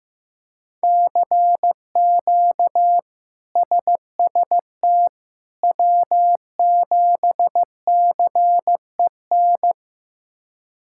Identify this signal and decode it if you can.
Morse code, heard as a keyed tone, about 15 wpm, 700 Hz. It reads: CQ SST W7CEN
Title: K1USN SST QSO Practice - 15wpm